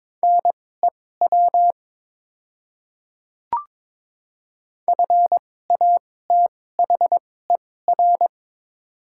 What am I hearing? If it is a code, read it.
Morse code, heard as a keyed tone, about 22 wpm, spaced as for 16 wpm, 700 Hz. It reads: NEW FATHER
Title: 2x Character Spacing - Top 200 Words - 22wpm